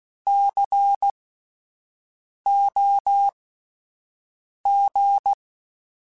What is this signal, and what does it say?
Morse code, heard as a keyed tone, about 16 wpm, spaced as for 5 wpm, 800 Hz. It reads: COG
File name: Spaced Alphas / Spaced Alphas mp3s / Cs sp @10